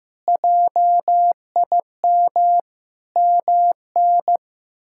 Morse code, heard as a keyed tone, about 15 wpm, 700 Hz. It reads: JIM MN
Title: K1USN SST QSO Practice - 15wpm